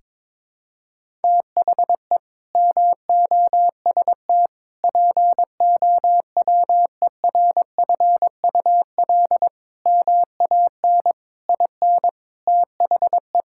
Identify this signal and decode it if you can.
Morse code, heard as a keyed tone, about 22 wpm, 700 Hz. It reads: THE MOST POWERFUL MAN IN THE
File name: practice / 22wpm / Sets of 7 Words - 22wpm